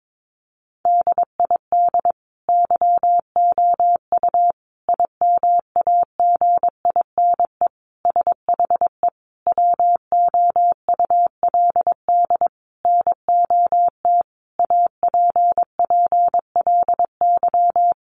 Morse code, heard as a keyed tone, about 22 wpm, 700 Hz. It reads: DID YOU IMAGINE SHE WOULD NOT APPLY
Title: Sentences from Top 1400 Words 22wpm